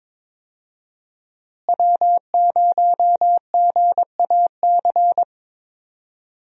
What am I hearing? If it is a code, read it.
Morse code, heard as a keyed tone, about 22 wpm, 700 Hz. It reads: W0GAC
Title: USA Call Signs - 5 Characters - Repeated 2x - 22wpm